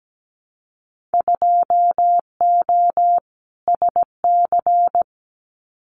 Morse code, heard as a keyed tone, about 17 wpm, 700 Hz. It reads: 2O SC
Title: Winter Field Day Exchanges Encore 17wpm